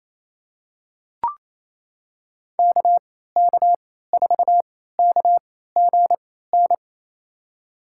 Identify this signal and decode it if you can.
Morse code, heard as a keyed tone, about 28 wpm, spaced as for 16 wpm, 700 Hz. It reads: KK4KGN
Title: USA Call Signs - 3x Character Spacing - 28wpm